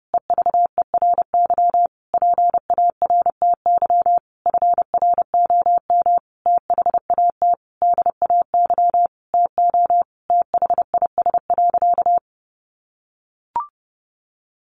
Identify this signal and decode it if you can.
Morse code, heard as a keyed tone, about 30 wpm, 700 Hz. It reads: EVERY PARTY FROM THAT DAY TO THIS.
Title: Sentences from Top 1400 Words 30wpm